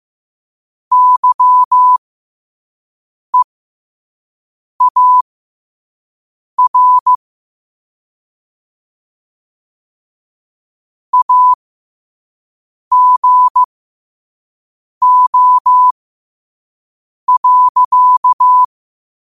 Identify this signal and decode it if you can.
Morse code, heard as a keyed tone, about 15 wpm, spaced as for 5 wpm, 1000 Hz. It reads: YEAR AGO.
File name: qotdfiles / 5wpm